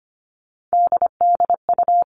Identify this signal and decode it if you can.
Morse code, heard as a keyed tone, about 25 wpm, 700 Hz. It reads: DDU
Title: All Cut Numbers - 3-Digit 25wpm